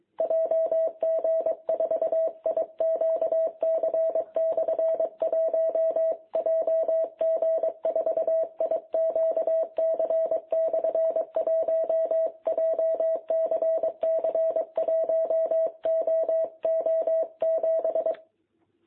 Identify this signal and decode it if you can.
Morse code, heard as a keyed tone, about 23 wpm, 660 Hz. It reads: JG4IQC/1 JG4IQC/1 JCC 1OO7
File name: jh8cbh / 2021cw